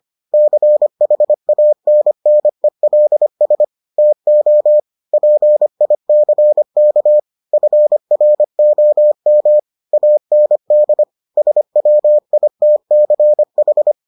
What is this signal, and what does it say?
Morse code, heard as a keyed tone, about 25 wpm, 600 Hz. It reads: CHANNELS TO PICK FROM AND SWITCH